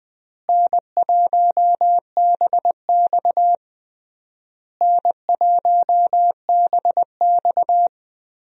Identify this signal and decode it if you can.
Morse code, heard as a keyed tone, about 19 wpm, 700 Hz. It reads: N1BX N1BX
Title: USA Call Signs - 4 Characters - Repeated 3x - 20wpm